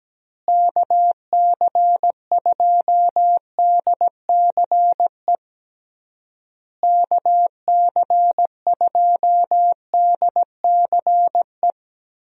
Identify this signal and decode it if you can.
Morse code, heard as a keyed tone, about 17 wpm, 700 Hz. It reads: KC2DCE KC2DCE
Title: USA Call Signs - 6 Characters - Repeated 3x - 17wpm